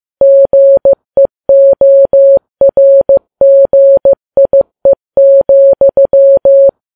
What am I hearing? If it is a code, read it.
Morse code, heard as a keyed tone, about 15 wpm, 550 Hz. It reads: GEORGIE,
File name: Files Web tls / Morse on9fv / morse15kps